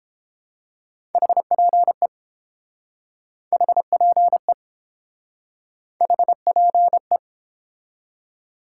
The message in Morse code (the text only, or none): HPE HPE HPE